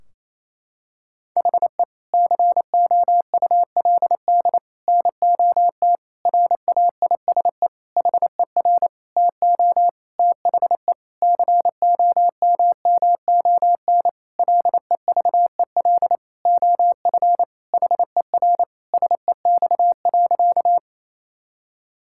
Morse code, heard as a keyed tone, about 28 wpm, 700 Hz. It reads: HE COULD NOT RAISE HER TO THE COMMON LEVEL OF HER SEX.
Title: Sentences from Top 1600 Words 28wpm